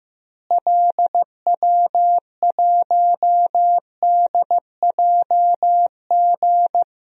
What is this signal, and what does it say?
Morse code, heard as a keyed tone, about 16 wpm, 700 Hz. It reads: LW1DJG
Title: DXCC-100 Call Signs - 6 Characters - Repeated 2x - 15wpm